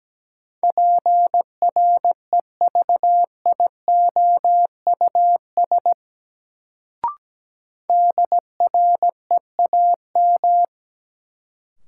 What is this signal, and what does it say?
Morse code, heard as a keyed tone, about 17 wpm, 700 Hz. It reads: PREVIOUS DREAM
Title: Sentences from Top 1500 Words - New Word Drill Alt - 17wpm